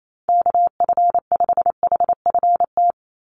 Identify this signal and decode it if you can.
Morse code, heard as a keyed tone, about 28 wpm, 700 Hz. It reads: KF5HFT
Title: USA Call Signs 28wpm